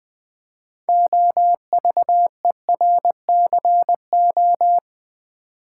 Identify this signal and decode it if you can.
Morse code, heard as a keyed tone, about 20 wpm, 700 Hz. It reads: OVERCO
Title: Build A Word - 3 Letters At A Time - 8-Letter Words - 20wpm